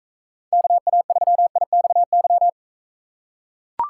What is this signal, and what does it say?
Morse code, heard as a keyed tone, about 39 wpm, 700 Hz. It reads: KA3IXY